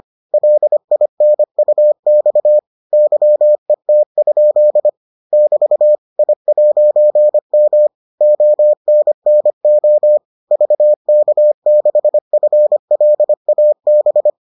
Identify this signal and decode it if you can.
Morse code, heard as a keyed tone, about 25 wpm, 600 Hz. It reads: LINUX YET? <BT> I'M ONNO VK6FLAB